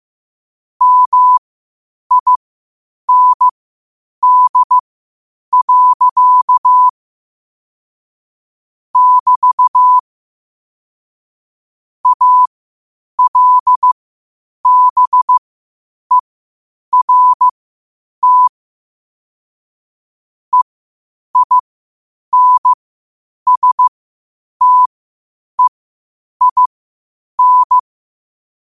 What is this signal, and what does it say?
Morse code, heard as a keyed tone, about 15 wpm, spaced as for 8 wpm, 1000 Hz. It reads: MIND. <BT> ALBERT EINSTEIN
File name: qotdfiles / 8wpm